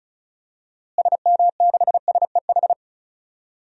Morse code, heard as a keyed tone, about 34 wpm, 700 Hz. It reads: SM6SEH
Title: DXCC-100 Call Signs - 6 Characters - Repeated 2x - 35wpm